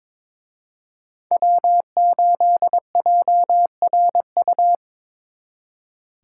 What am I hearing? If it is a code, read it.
Morse code, heard as a keyed tone, about 26 wpm, 700 Hz. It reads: W8JRU
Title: Speed Racing - USA 1x3 Call Signs 22wpm